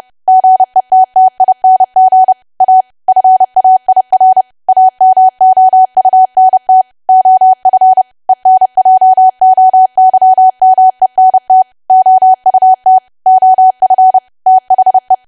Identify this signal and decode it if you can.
Morse code, heard as a keyed tone, about 30 wpm, 750 Hz. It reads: GETTING A FAIR AMOUNT OF ENJOYMENT OUT OF THE
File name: cwa / W1AW-Level3 / 30WPM